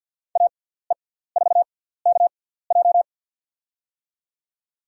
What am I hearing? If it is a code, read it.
Morse code, heard as a keyed tone, about 47 wpm, spaced as for 17 wpm, 690 Hz. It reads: AE4KJ
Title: USA Call Signs - 5 Characters - 6x Character Spacing - 50wpm